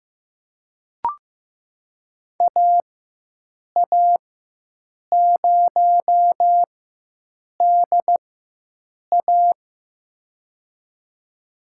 Morse code, heard as a keyed tone, about 15 wpm, spaced as for 7 wpm, 700 Hz. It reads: AA0DA
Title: USA Call Signs - 4x Character Spacing - 15wpm